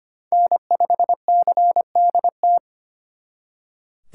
Morse code, heard as a keyed tone, about 32 wpm, 700 Hz. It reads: N5CDT